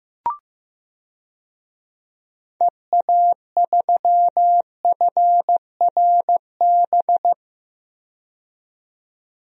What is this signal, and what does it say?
Morse code, heard as a keyed tone, about 15 wpm, 700 Hz. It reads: EA3FRB
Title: DXCC-100 Call Signs - 6 Characters - 15wpm